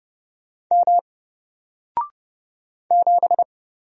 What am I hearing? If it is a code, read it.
Morse code, heard as a keyed tone, about 29 wpm, 700 Hz. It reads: M 7
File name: Single-Letter-Number-30wpm